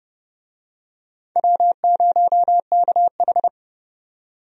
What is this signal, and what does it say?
Morse code, heard as a keyed tone, about 30 wpm, 700 Hz. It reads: W0KH
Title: USA Call Signs - 4 Characters - 30wpm